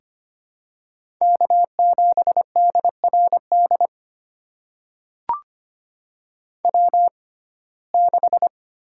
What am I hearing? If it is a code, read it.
Morse code, heard as a keyed tone, about 25 wpm, spaced as for 18 wpm, 700 Hz. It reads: K7DRD W6
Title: USA Call Signs - 6x Character Spacing - 25wpm